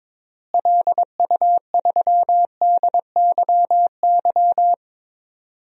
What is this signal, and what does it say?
Morse code, heard as a keyed tone, about 22 wpm, 700 Hz. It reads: LU3DYY
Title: DXCC-100 Call Signs - 6 Characters - Repeated 2x - 22wpm